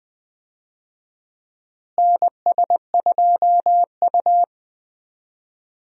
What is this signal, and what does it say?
Morse code, heard as a keyed tone, about 20 wpm, 700 Hz. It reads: NS2U
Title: USA Call Signs - 4 Characters - Repeated 2x - 20wpm